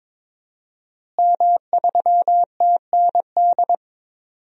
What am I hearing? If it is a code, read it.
Morse code, heard as a keyed tone, about 21 wpm, 700 Hz. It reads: M3TND
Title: DXCC-100 Call Signs - 5 Characters - Repeated 2x - 22wpm